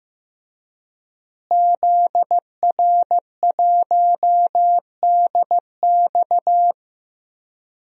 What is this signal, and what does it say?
Morse code, heard as a keyed tone, about 15 wpm, 700 Hz. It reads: ZR1DX
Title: DXCC-100 Call Signs - 5 Characters - Repeated 3x - 15wpm